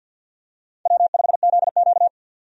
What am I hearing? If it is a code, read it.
Morse code, heard as a keyed tone, about 52 wpm, 700 Hz. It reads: W5ZQ
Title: USA Call Signs - 4 Characters - 50wpm